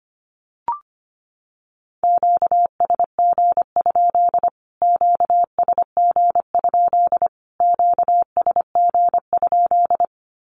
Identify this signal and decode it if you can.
Morse code, heard as a keyed tone, about 25 wpm, 700 Hz. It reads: QSG? QSG? QSG?
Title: Q-Codes Repeated 3x 25wpm